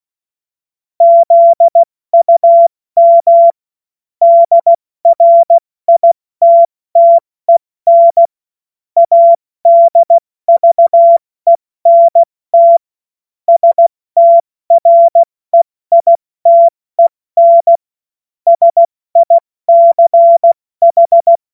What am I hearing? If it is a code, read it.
Morse code, heard as a keyed tone, about 16 wpm, 680 Hz. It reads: ZUM DRITTEN ADVENT STREITEN SICH
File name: Morsekurs / Witze / Witz 17. Januar 2023 70 F80